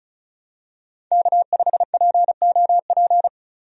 Morse code, heard as a keyed tone, about 36 wpm, 700 Hz. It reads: K5POP
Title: USA Call Signs - 5 Characters - 35wpm